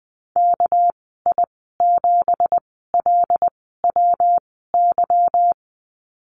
Morse code, heard as a keyed tone, about 20 wpm, spaced as for 15 wpm, 700 Hz. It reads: KI7LWY
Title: 207 Lesson 33g - Letter j - Call Signs 20wpm